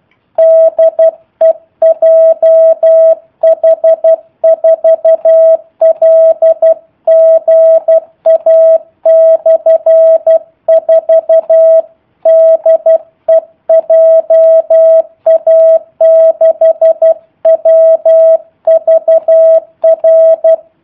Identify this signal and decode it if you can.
Morse code, heard as a keyed tone, about 12 wpm, 650 Hz. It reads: DE JH4LGA/4 DE JA6WVR
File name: jh8cbh / 2019cw / sp03- w